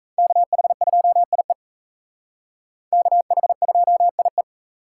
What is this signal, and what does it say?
Morse code, heard as a keyed tone, about 35 wpm, 700 Hz. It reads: KH2IE KH2IE
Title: Speed Racing - USA 2x2 Call Signs 28wpm